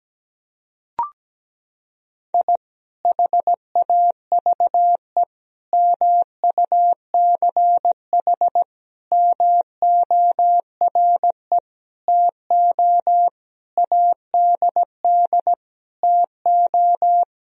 Morse code, heard as a keyed tone, about 17 wpm, 700 Hz. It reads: I HAVE MUCH MORE TO ADD TO
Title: Sentences from Top 1000 Words - Review Part 1 - 17wpm